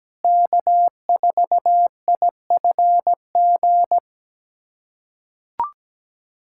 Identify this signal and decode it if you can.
Morse code, heard as a keyed tone, about 17 wpm, 700 Hz. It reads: K4IFG
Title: USA Callsign DE Callsign - 5 Characters - 17wpm